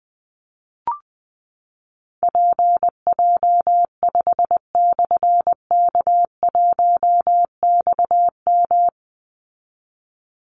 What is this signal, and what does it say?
Morse code, heard as a keyed tone, about 20 wpm, 700 Hz. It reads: PJ5/K1XM
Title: Call Signs Repeated 3x - 20wpm